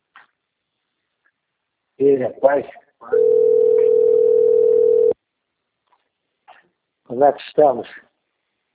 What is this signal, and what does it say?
Ringback tone (North America), 440 + 480 Hz, with speech.
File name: out- - - - - .11777